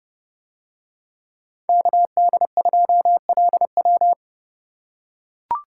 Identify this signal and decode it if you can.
Morse code, heard as a keyed tone, about 28 wpm, 700 Hz. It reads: KD2LW